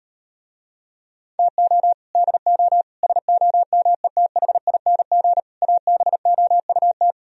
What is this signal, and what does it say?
Morse code, heard as a keyed tone, about 38 wpm, 700 Hz. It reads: TO DO SOMETHING ABOUT